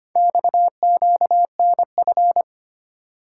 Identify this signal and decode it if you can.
Morse code, heard as a keyed tone, about 25 wpm, 700 Hz. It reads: XQNF
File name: practice / 25wpm / Four Letter-Number Combinations - 25wpm